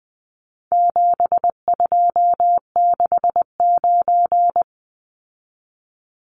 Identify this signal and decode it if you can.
Morse code, heard as a keyed tone, about 20 wpm, 700 Hz. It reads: 7269
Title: Numbers - 4-Digit 20wpm